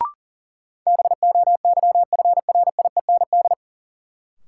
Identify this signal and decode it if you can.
Morse code, heard as a keyed tone, about 40 wpm, 700 Hz. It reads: BOYFRIEND
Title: Sentences from Top 1000 Words - Review Part 5 - New Word Drill - 40wpm